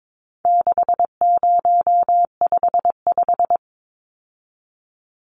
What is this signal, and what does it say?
Morse code, heard as a keyed tone, about 22 wpm, 700 Hz. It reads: 6055
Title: Numbers - 4-Digit 22wpm